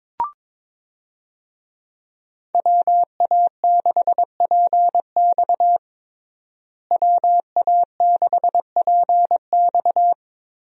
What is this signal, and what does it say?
Morse code, heard as a keyed tone, about 22 wpm, 700 Hz. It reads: WA6PX WA6PX
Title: USA Call Signs - 5 Characters - Repeated 3x - 22wpm